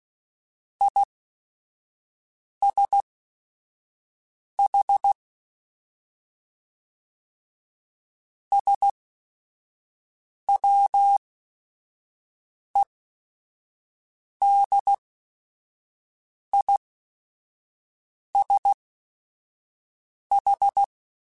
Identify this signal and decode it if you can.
Morse code, heard as a keyed tone, about 16 wpm, spaced as for 5 wpm, 800 Hz. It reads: ISH SWEDISH
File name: General / General-mp3s / ISH @5